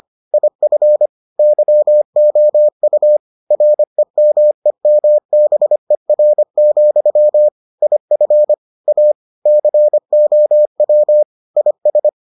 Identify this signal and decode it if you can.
Morse code, heard as a keyed tone, about 25 wpm, 600 Hz. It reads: IF YOU REMEMBER, IF A COW IS